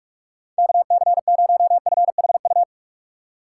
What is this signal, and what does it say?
Morse code, heard as a keyed tone, about 47 wpm, 700 Hz. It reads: KC0FHU
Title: USA Call Signs - 6 Characters - 6x Character Spacing - 45wpm